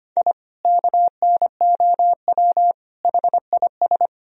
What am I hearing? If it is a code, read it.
Morse code, heard as a keyed tone, about 25 wpm, 700 Hz. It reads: I KNOW HIS
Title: Sentences from Top 1000 Words - Review Part 3 - 25wpm